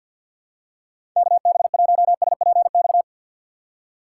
Morse code, heard as a keyed tone, about 49 wpm, 700 Hz. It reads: KB1SPX
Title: USA Call Signs - 6 Characters - 4x Character Spacing - 50wpm